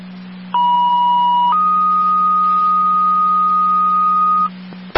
Two-tone page, heard as a one-second tone, then a three-second tone, about 970 then 1240 Hz. It is heard from 0.5 to 4.5 s.